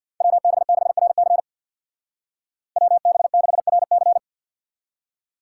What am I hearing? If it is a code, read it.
Morse code, heard as a keyed tone, about 60 wpm, 700 Hz. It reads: WB6RC WB6RC
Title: Speed Racing - USA Call Signs 50wpm